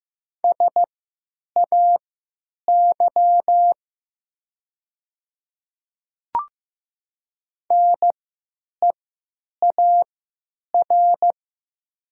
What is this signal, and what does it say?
Morse code, heard as a keyed tone, about 15 wpm, spaced as for 9 wpm, 700 Hz. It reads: SAY NEAR
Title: 3x Character Spacing - Top 300 Words - 15wpm